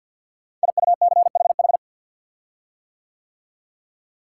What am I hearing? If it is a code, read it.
Morse code, heard as a keyed tone, about 53 wpm, 700 Hz. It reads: IUCHH
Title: Code Groups - 5 Letters - 50wpm